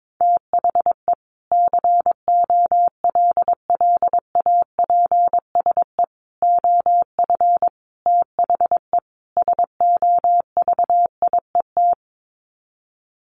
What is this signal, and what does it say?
Morse code, heard as a keyed tone, about 22 wpm, 700 Hz. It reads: THE COLLAPSE OF THE SOVIET
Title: Sets of 5 Words Encore 22wpm